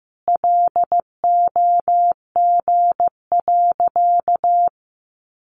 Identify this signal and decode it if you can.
Morse code, heard as a keyed tone, about 14 wpm, 700 Hz. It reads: LOG.
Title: Sentences from Top 1500 Words 15wpm